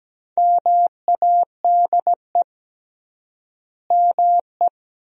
Morse code, heard as a keyed tone, about 17 wpm, 700 Hz. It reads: MADE ME
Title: 3x Word-Spacing - Sentences from Top 400 Words - 17wpm